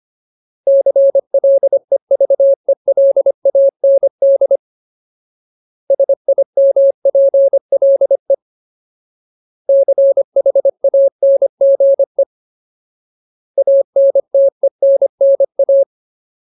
Morse code, heard as a keyed tone, about 25 wpm, 550 Hz. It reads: CLEVELAND SIMPLE CHANGE ANTENNA
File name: Ultimate Word List / 25WPM Ultimate